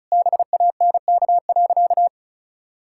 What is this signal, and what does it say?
Morse code, heard as a keyed tone, about 36 wpm, 700 Hz. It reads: BANK.
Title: Sentences from Top 1000 Words - Review Part 6 - 35wpm